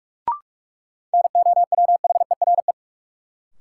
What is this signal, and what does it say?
Morse code, heard as a keyed tone, about 46 wpm, 700 Hz. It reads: NOWHERE
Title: Sets of 5 Words Encore - New Word Drill - 45wpm